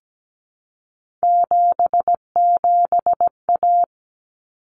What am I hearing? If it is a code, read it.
Morse code, heard as a keyed tone, about 17 wpm, 700 Hz. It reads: 77A
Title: Common Cut Numbers - 3-Digit 17wpm